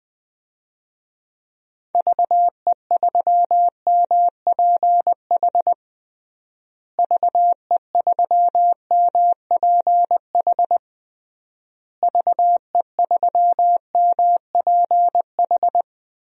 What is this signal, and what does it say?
Morse code, heard as a keyed tone, about 20 wpm, 700 Hz. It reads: VE3MPH VE3MPH VE3MPH
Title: DXCC-100 Call Signs - 6 Characters - Repeated 3x - 20wpm